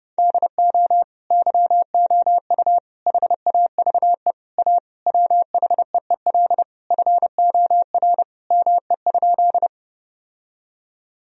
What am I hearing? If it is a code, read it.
Morse code, heard as a keyed tone, about 30 wpm, 700 Hz. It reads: DO YOU HAVE A WHEEL FOR ME?
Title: Sentences from Top 1000 Words - Review Part 3 - 30wpm